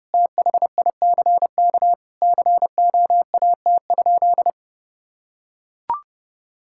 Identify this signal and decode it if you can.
Morse code, heard as a keyed tone, about 31 wpm, 700 Hz. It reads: THICK COAT?
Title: Sentences from Top 1600 Words 30wpm